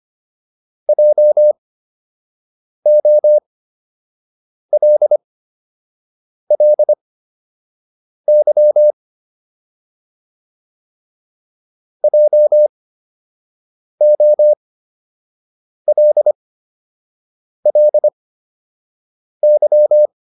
Morse code, heard as a keyed tone, about 25 wpm, spaced as for 6 wpm, 600 Hz. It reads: JOLLY JOLLY